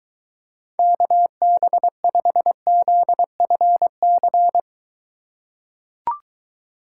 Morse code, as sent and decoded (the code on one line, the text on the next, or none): -.- -... ..... --.. ..-. -.-.
KB5ZFC